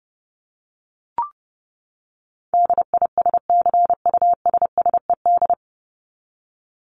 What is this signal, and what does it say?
Morse code, heard as a keyed tone, about 30 wpm, 700 Hz. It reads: DISCUSSED
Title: ED Suffix - Top 100 Words with Base Words 30wpm